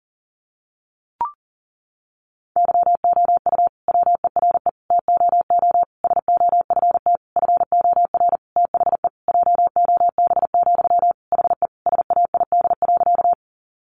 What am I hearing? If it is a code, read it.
Morse code, heard as a keyed tone, about 40 wpm, 700 Hz. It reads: YOU WERE TOO SOFT FOR THE JOB, HE SAID.
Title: Sentences from Top 1500 Words 40wpm